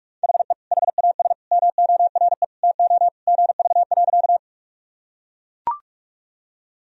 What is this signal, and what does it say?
Morse code, heard as a keyed tone, about 42 wpm, 700 Hz. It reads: HE HAS MORE TO GV.
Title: Sentences from Top 200 Words and Abbreviations 45wpm